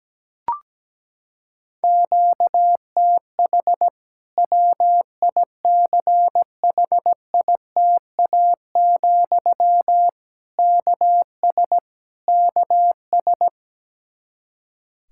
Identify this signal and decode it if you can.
Morse code, heard as a keyed tone, about 17 wpm, 700 Hz. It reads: QTH WICHITA, KS KS
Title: K1USN SST QSO Practice 17wpm